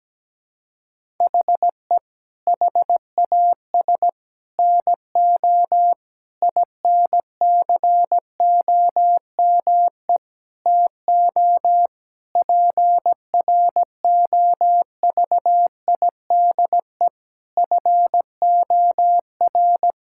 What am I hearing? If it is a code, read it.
Morse code, heard as a keyed tone, about 17 wpm, 700 Hz. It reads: HE HAS NO INCOME TO PROVIDE FOR
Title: Sentences from Top 1000 Words - Review Part 5 - 17wpm